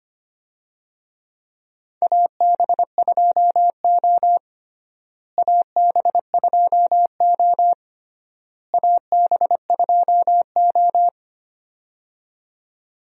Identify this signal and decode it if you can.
Morse code, heard as a keyed tone, about 25 wpm, 700 Hz. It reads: AB2O AB2O AB2O
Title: USA Call Signs - 4 Characters - Repeated 3x - 25wpm